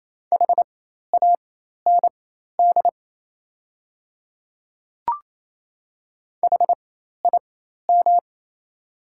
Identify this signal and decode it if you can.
Morse code, heard as a keyed tone, about 28 wpm, spaced as for 13 wpm, 700 Hz. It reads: HAND HIM
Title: 4x Character Spacing - Top 200 Words - 28wpm